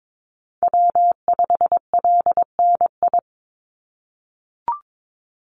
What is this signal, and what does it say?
Morse code, heard as a keyed tone, about 22 wpm, 700 Hz. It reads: W5LNI
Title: Call Signs - 22wpm